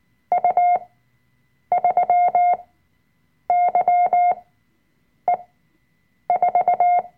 Morse code, heard as a keyed tone, about 19 wpm, spaced as for 7 wpm, 710 Hz. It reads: U3YE4
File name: Workshop Dateien / CW-Online-Kurs / Lektion 005